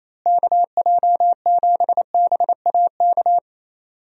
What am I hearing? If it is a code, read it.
Morse code, heard as a keyed tone, about 28 wpm, 700 Hz. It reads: KJ7BAK
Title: USA Call Signs - 6 Characters - Repeated 2x - 28wpm